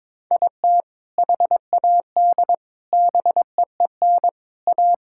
Morse code, heard as a keyed tone, about 22 wpm, 700 Hz. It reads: IT HAD BEEN A
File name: practice / 22wpm / Sentences with 3-Word Phrases - 22wpm